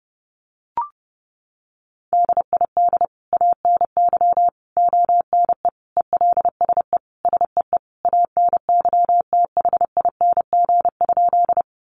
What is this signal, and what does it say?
Morse code, heard as a keyed tone, about 30 wpm, 700 Hz. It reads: DID ANY ONE ELSE SEE ANYTHING?
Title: Sentences from Top 1000 Words 30wpm